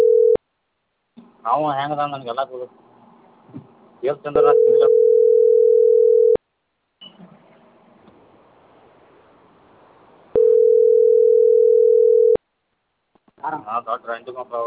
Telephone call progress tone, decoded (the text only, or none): ringback tone